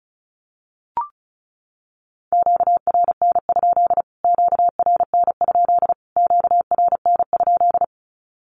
Morse code, heard as a keyed tone, about 34 wpm, 700 Hz. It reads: QRN? QRN? QRN?
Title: Q-Codes Repeated 3x 35wpm